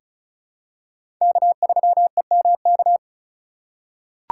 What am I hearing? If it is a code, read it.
Morse code, heard as a keyed tone, about 34 wpm, 700 Hz. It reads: K3EMK
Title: USA Call Signs - 5 Characters - 6x Character Spacing - 35wpm